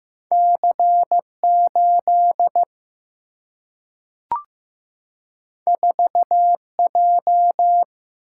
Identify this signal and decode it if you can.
Morse code, heard as a keyed tone, about 15 wpm, 700 Hz. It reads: C 8 4 J
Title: Single-Letter Single-Number 15wpm